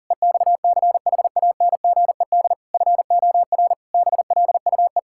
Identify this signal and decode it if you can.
Morse code, heard as a keyed tone, about 37 wpm, 700 Hz. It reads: EXCHANGED FOR BLUE
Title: Pangram Challenge v2 - 40wpm